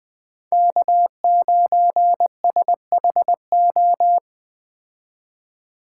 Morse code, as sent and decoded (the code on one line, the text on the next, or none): -.- ----. ... .... ---
K9SHO